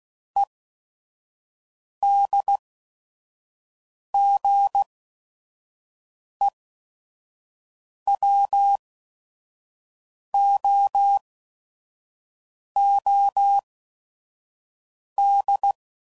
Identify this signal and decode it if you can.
Morse code, heard as a keyed tone, about 16 wpm, spaced as for 5 wpm, 800 Hz. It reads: EDGEWOOD
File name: CW Code Groups @5WPM / fromSM's / Cities